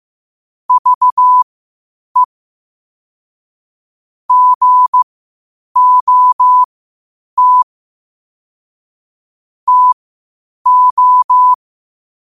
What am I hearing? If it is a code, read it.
Morse code, heard as a keyed tone, about 15 wpm, spaced as for 8 wpm, 1000 Hz. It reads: VE GOT TO
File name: qotdfiles / 8wpm